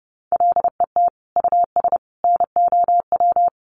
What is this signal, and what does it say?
Morse code, heard as a keyed tone, about 30 wpm, 700 Hz. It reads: LET US NOW
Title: Sentences from Top 1500 Words 30wpm